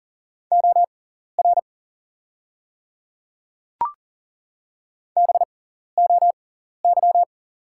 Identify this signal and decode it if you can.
Morse code, heard as a keyed tone, about 40 wpm, spaced as for 14 wpm, 690 Hz. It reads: OR BOY
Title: 6x Character Spacing - Top 200 Words - 40wpm